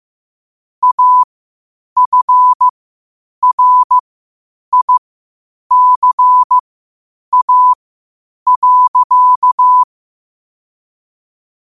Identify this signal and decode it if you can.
Morse code, heard as a keyed tone, about 15 wpm, spaced as for 8 wpm, 1000 Hz. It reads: AFRICA.
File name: qotdfiles / 8wpm